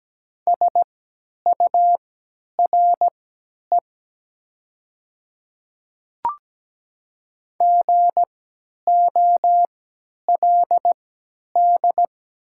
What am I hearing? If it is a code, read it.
Morse code, heard as a keyed tone, about 17 wpm, spaced as for 10 wpm, 700 Hz. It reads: SURE GOLD
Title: 3x Character Spacing - Top 500 Words - 17wpm